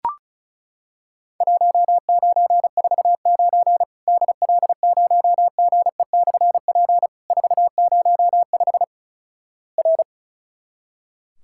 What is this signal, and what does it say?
Morse code, heard as a keyed tone, about 34 wpm, 700 Hz. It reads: 1949 DL0GE/P 405 R
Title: Worked All Europe DX Contest - QTC Exchange - 35wpm